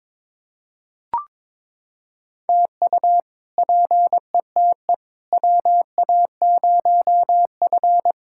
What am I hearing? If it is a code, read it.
Morse code, heard as a keyed tone, about 22 wpm, 700 Hz. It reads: TU PETE WA0F
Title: K1USN SST QSO Practice - 22wpm